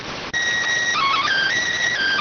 SSTV (slow-scan television), PD 290.